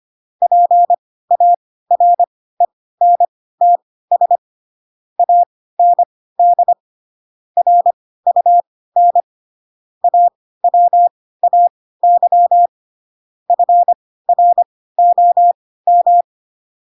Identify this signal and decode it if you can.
Morse code, heard as a keyed tone, about 25 wpm, spaced as for 16 wpm, 700 Hz. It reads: PARENTS AND RUN AWAY FROM
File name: Pinocchio / Pinocchio ch4 10 25 ik1hge